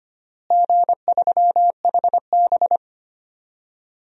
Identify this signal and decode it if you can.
Morse code, heard as a keyed tone, about 25 wpm, 700 Hz. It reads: G3HB
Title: DXCC-100 Call Signs - 4 Characters - Repeated 2x - 25wpm